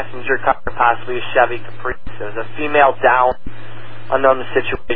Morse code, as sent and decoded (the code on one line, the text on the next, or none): none